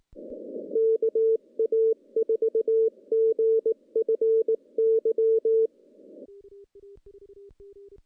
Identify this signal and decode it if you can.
Morse code, heard as a keyed tone, about 18 wpm, 440 Hz. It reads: KA4GFY
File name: contests / CW-SS-k5tr- -0200